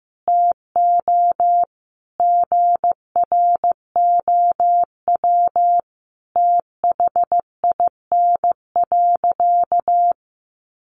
Morse code, heard as a keyed tone, about 15 wpm, 700 Hz. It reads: TO GROW THIN.
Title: Sentences from Top 1200 Words 15wpm